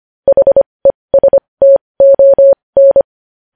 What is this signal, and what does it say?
Morse code, heard as a keyed tone, about 25 wpm, spaced as for 20 wpm, 550 Hz. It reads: HESTON
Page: News headlines in Morse code at 20 WPM